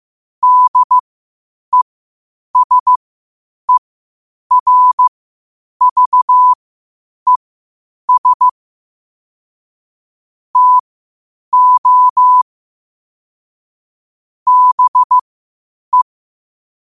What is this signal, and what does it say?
Morse code, heard as a keyed tone, about 15 wpm, spaced as for 8 wpm, 1000 Hz. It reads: DESERVES TO BE
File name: 8wpm